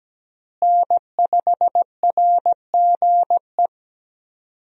Morse code, heard as a keyed tone, about 17 wpm, 700 Hz. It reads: N5RGE
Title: USA Call Signs - 5 Characters - Repeated 2x - 17wpm